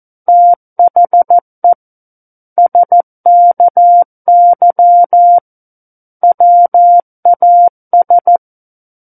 Morse code, heard as a keyed tone, about 14 wpm, 700 Hz. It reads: THE SKY WAS
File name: Book One / 05-Chapter 5 - 14 WPM